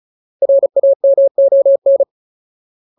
Morse code, heard as a keyed tone, about 36 wpm, 550 Hz. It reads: RAMON
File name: Ultimate Word List / 35WPM Ultimate